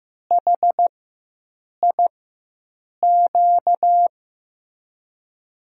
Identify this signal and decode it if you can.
Morse code, heard as a keyed tone, about 15 wpm, spaced as for 7 wpm, 700 Hz. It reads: HIQ
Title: USA Call Signs - 6 Characters - 4x Character Spacing - 15wpm